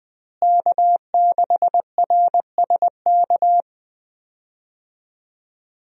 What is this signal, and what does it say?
Morse code, heard as a keyed tone, about 20 wpm, 700 Hz. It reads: K6RSK
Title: USA Call Signs - 5 Characters - Repeated 2x - 20wpm